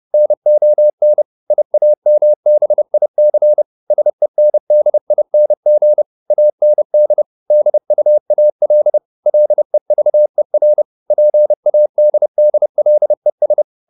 Morse code, heard as a keyed tone, about 30 wpm, 600 Hz. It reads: NON IAMBIC SENDING AND DUAL LEVER PADDLES
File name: SS301 30